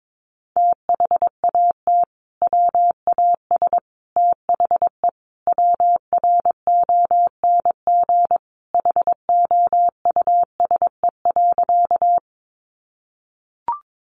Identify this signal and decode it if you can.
Morse code, heard as a keyed tone, about 22 wpm, 700 Hz. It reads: THAT WAS THE WRONG HOUSE.
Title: Sentences from Top 900 Words 22wpm